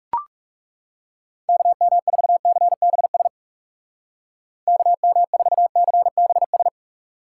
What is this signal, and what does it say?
Morse code, heard as a keyed tone, about 40 wpm, 700 Hz. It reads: KM4CBS KM4CBS